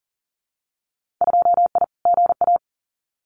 Morse code, heard as a keyed tone, about 42 wpm, 700 Hz. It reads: 2I GA
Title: Winter Field Day Exchanges 40wpm